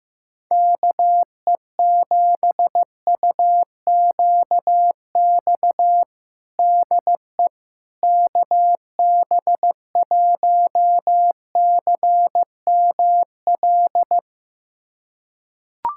Morse code, heard as a keyed tone, about 15 wpm, 700 Hz. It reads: KE7UQX DE KB1CML
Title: USA Callsign DE Callsign 15wpm